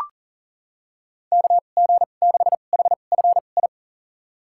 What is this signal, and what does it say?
Morse code, heard as a keyed tone, about 38 wpm, spaced as for 28 wpm, 700 Hz. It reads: KG6HFI
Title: USA Call Signs - 6 Characters - 2x Character Spacing - 40wpm